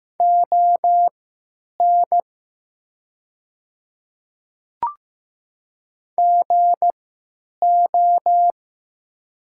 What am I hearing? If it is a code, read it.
Morse code, heard as a keyed tone, about 15 wpm, spaced as for 9 wpm, 700 Hz. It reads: ON GO
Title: 3x Character Spacing - Top 300 Words - 15wpm